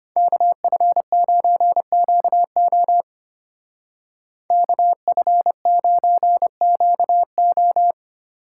Morse code, heard as a keyed tone, about 32 wpm, 700 Hz. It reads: KF9QO KF9QO